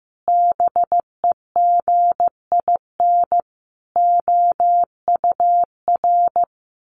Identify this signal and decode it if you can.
Morse code, heard as a keyed tone, about 14 wpm, 700 Hz. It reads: BEGIN OUR
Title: Sentences from Top 1300 Words 15wpm